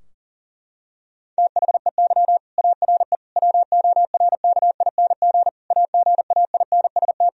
Morse code, heard as a keyed tone, about 37 wpm, 700 Hz. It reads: THEY ARE WORKING AGAINST
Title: Sentences from Top 1000 Words - Review Part 4 - 40wpm